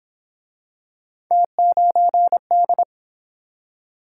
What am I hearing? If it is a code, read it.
Morse code, heard as a keyed tone, about 28 wpm, 700 Hz. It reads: T9D